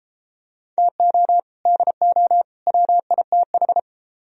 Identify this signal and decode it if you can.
Morse code, heard as a keyed tone, about 33 wpm, 700 Hz. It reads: TO DO WITH